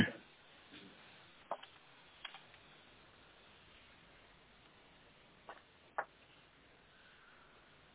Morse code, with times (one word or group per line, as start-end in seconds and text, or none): none